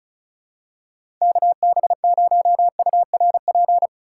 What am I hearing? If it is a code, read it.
Morse code, heard as a keyed tone, about 34 wpm, 700 Hz. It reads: KB0URP
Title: USA Call Signs - 6 Characters - 2x Character Spacing - 35wpm